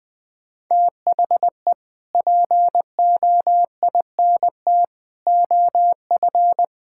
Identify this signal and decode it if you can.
Morse code, heard as a keyed tone, about 20 wpm, 700 Hz. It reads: THE POINT OF
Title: Build A Phrase - 5-Word Phrases - 20wpm